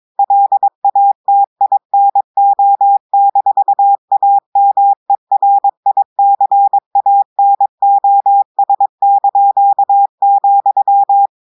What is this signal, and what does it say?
Morse code, heard as a keyed tone, about 22 wpm, 800 Hz. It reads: LATINO-AMERICANOS),
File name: audios / portugais / portugais texte en clair 1 vitesse 22 mots minutes